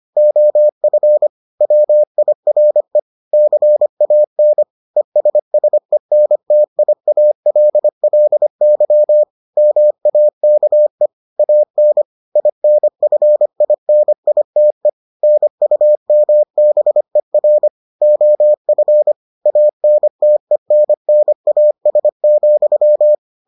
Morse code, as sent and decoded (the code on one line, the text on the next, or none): --- ..-. / .-- .. .-. / -.-. .- -. / . ... ... -. - .. .- .-.. .-.. -.-- / -- .- -.- . / .- -. / .. -. ..-. .. -. .. - . / -. ..- -- -... .-. / --- ..-. / .- -. - . -. -. .- ... --..--
OF WIRE CAN ESSENTIALLY MAKE AN INFINITE NUMBER OF ANTENNAS,